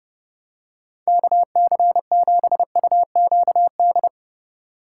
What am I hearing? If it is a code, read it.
Morse code, heard as a keyed tone, about 30 wpm, 700 Hz. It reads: KC7UQD